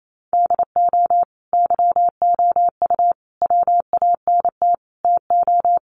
Morse code, heard as a keyed tone, about 28 wpm, 700 Hz. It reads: DO YOU WANT TO
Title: Sentences from Top 700 Words 28wpm